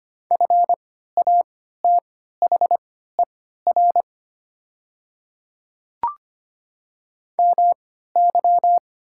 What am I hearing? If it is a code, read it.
Morse code, heard as a keyed tone, about 25 wpm, spaced as for 14 wpm, 700 Hz. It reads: FATHER MY
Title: 3x Character Spacing - Top 200 Words - 25wpm